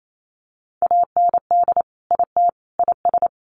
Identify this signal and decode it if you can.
Morse code, heard as a keyed tone, about 28 wpm, 700 Hz. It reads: AND IT IS
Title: Sets of 3 Words 28wpm